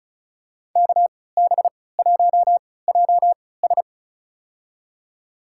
Morse code, heard as a keyed tone, about 34 wpm, spaced as for 20 wpm, 700 Hz. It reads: KB1JS
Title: USA Call Signs - 5 Characters - 3x Character Spacing - 35wpm